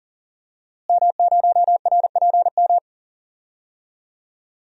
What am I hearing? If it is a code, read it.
Morse code, heard as a keyed tone, about 40 wpm, 700 Hz. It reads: M0RPM
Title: DXCC-100 Call Signs - 5 Characters - Repeated 3x - 40wpm